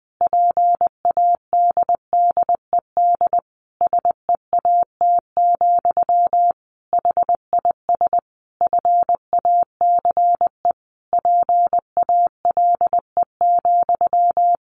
Morse code, heard as a keyed tone, about 20 wpm, 700 Hz. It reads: PADDED SEAT, HIS FACE PALE,
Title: A Sound Of Thunder 20wpm